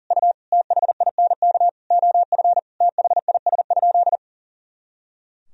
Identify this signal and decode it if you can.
Morse code, heard as a keyed tone, about 40 wpm, 700 Hz. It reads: U THINK OF THIS?
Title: Sentences from Top 200 Words and Abbreviations 40wpm